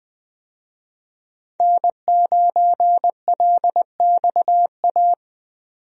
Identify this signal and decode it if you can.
Morse code, heard as a keyed tone, about 20 wpm, 700 Hz. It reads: N9LXA
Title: USA Call Signs - 5 Characters - 20wpm